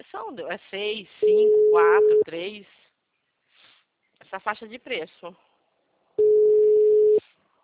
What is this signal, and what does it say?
Ringback tone (Europe, 425 Hz), with speech.
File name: out- -2001- - - .32460